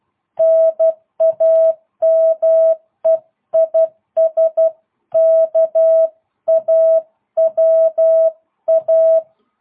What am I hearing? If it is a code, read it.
Morse code, heard as a keyed tone, about 12 wpm, 660 Hz. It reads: NAME IS KAWA